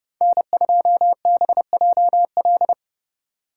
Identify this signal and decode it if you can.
Morse code, heard as a keyed tone, about 30 wpm, 700 Hz. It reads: N2BJL